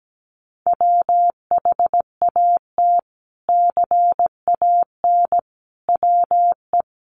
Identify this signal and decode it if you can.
Morse code, heard as a keyed tone, about 17 wpm, 700 Hz. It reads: WHAT CAN WE
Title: Sentences from Top 100 Words 17wpm